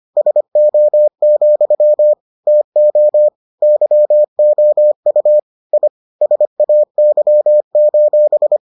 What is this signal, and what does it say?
Morse code, heard as a keyed tone, about 25 wpm, 600 Hz. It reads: SO, TO YOU I SAY: